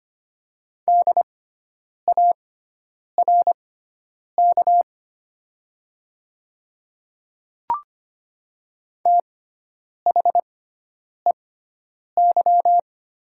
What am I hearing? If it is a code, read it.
Morse code, heard as a keyed tone, about 25 wpm, spaced as for 9 wpm, 700 Hz. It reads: DARK THEY
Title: 6x Character Spacing - Top 500 Words - 25wpm